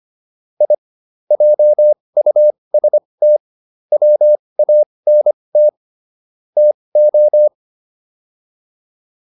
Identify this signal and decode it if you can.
Morse code, heard as a keyed tone, about 25 wpm, spaced as for 20 wpm, 600 Hz. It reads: I JUST WANT TO